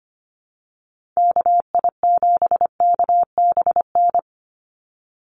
Morse code, heard as a keyed tone, about 25 wpm, 700 Hz. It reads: KI7KBN
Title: USA Call Signs 25wpm